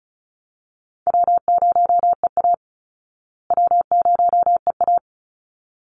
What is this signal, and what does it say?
Morse code, heard as a keyed tone, about 34 wpm, 700 Hz. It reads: W0EA W0EA
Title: Easy Call Signs Repeated 2x - 35wpm